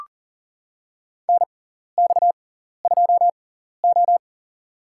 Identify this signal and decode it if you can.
Morse code, heard as a keyed tone, about 37 wpm, spaced as for 14 wpm, 690 Hz. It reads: NX2O
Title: USA Call Signs - 6x Character Spacing - 40wpm